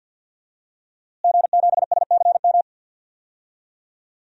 Morse code, heard as a keyed tone, about 48 wpm, 700 Hz. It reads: G7SCM
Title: DXCC-100 Call Signs - 5 Characters - Repeated 2x - 50wpm